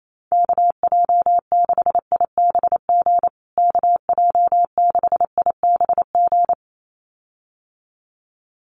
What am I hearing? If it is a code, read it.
Morse code, heard as a keyed tone, about 28 wpm, 700 Hz. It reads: KJ6IBG KJ6IBG
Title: USA Call Signs Repeated 2x 28wpm